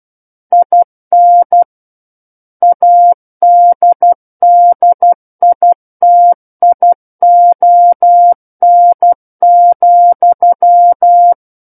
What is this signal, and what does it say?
Morse code, heard as a keyed tone, about 12 wpm, 700 Hz. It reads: IN ADDITION,
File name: Book One / 03-Chapter 3 - 12 WPM